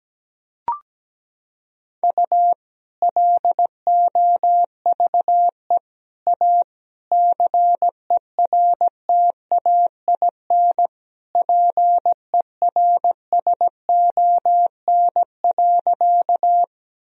Morse code, heard as a keyed tone, about 17 wpm, 700 Hz. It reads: U LOVE A CERTAIN PERSON.
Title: Sentences from Top 500 Words and Abbreviations 17wpm